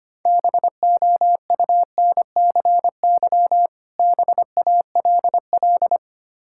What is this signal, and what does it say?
Morse code, heard as a keyed tone, about 25 wpm, 700 Hz. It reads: BOUNCY BALL
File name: practice / 25wpm / Difficult Words with Hints - 25wpm